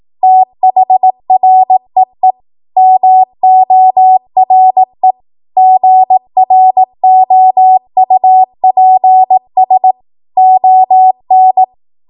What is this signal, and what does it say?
Morse code, heard as a keyed tone, about 18 wpm, 750 Hz. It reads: THREE MORE GROUPS ON
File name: Archive / 18 WPM / 18WPM